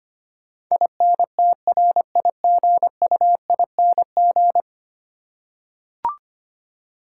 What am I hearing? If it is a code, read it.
Morse code, heard as a keyed tone, about 25 wpm, 700 Hz. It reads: INTRIGUING
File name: 10 Letter Words - First 3 Letters Given - 25wpm